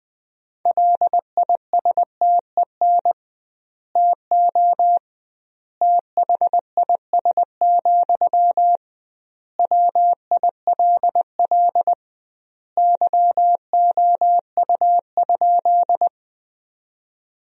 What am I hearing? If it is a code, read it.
Morse code, heard as a keyed tone, about 20 wpm, 700 Hz. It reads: LISTEN TO THIS, WILL YOU?
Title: 2x Word-Spacing - Sentences from Top 400 Words - 20wpm